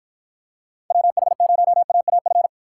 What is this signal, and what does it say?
Morse code, heard as a keyed tone, about 50 wpm, 700 Hz. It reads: WH0AAF